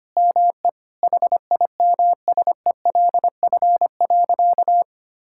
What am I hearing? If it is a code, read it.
Morse code, heard as a keyed tone, about 25 wpm, 700 Hz. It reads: ME HIMSELF.
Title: Sentences from Top 1600 Words 25wpm